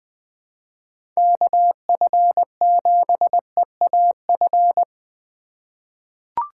Morse code, heard as a keyed tone, about 20 wpm, 700 Hz. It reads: KF7EAF
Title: USA Call Signs - 6 Characters - 3x Character Spacing - 20wpm